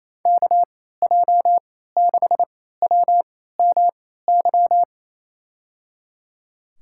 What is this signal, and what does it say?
Morse code, heard as a keyed tone, about 28 wpm, spaced as for 16 wpm, 700 Hz. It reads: KJ6WMY
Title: USA Call Signs - 6 Characters - 3x Character Spacing - 28wpm